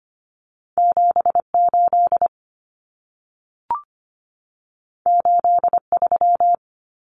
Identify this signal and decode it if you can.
Morse code, heard as a keyed tone, about 25 wpm, 700 Hz. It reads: 78 83
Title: Numbers - 2-Digit 25wpm